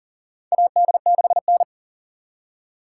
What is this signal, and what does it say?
Morse code, heard as a keyed tone, about 37 wpm, 700 Hz. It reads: AD6N
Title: USA Call Signs - 4 Characters - 2x Character Spacing - 40wpm